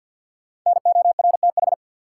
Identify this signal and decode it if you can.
Morse code, heard as a keyed tone, about 48 wpm, 700 Hz. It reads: NORTH
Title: Sentences from Top 400 Words - New Word Drill - 50wpm